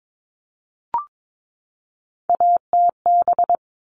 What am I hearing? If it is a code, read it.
Morse code, heard as a keyed tone, about 22 wpm, 700 Hz. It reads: ATB
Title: All Cut Numbers - 3-Digit 22wpm